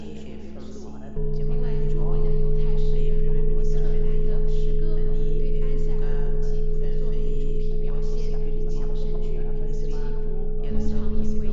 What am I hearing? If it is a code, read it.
Dial tone (Europe, 425 Hz), with speech.